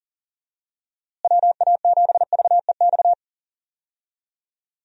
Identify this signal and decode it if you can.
Morse code, heard as a keyed tone, about 40 wpm, 700 Hz. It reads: WA7VEX
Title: USA Call Signs - 6 Characters - Repeated 2x - 40wpm